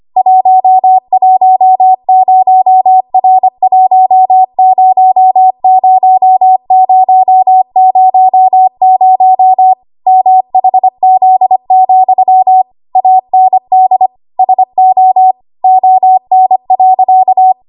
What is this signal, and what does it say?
Morse code, heard as a keyed tone, about 25 wpm, 750 Hz. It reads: 110R100000 MHZ, AND SO ON.
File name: Archive / 25 WPM / 25WPM